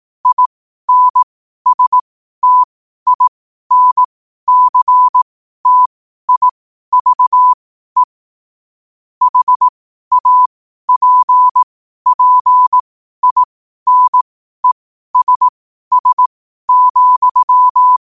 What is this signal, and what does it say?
Morse code, heard as a keyed tone, about 18 wpm, spaced as for 12 wpm, 1000 Hz. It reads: INSTINCTIVE HAPPINESS,
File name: qotdfiles / 12wpm